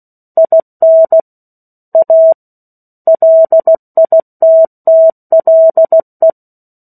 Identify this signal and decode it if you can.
Morse code, heard as a keyed tone, about 16 wpm, 650 Hz. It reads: IN A LITTLE
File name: 03-Chapter-03 16 WPM